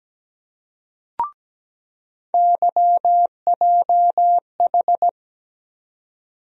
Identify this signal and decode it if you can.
Morse code, heard as a keyed tone, about 17 wpm, 700 Hz. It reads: YJH
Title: USA Call Sign Postfixes - 3 Letters 17wpm